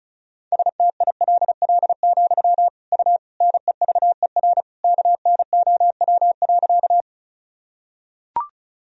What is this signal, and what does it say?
Morse code, heard as a keyed tone, about 36 wpm, 700 Hz. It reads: STILL, U NEVER KNOW.
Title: Sentences from Top 300 Words and Abbreviations 35wpm